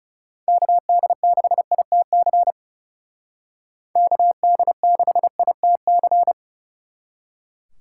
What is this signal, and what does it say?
Morse code, heard as a keyed tone, about 37 wpm, 700 Hz. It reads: KD6ITC KD6ITC
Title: Speed Racing - USA Call Signs 30wpm